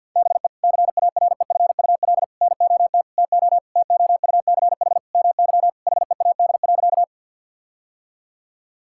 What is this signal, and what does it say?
Morse code, heard as a keyed tone, about 50 wpm, 700 Hz. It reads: BE CAREFUL NOT TO TOUCH MY HEAD.
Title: Sentences from Top 1600 Words 50wpm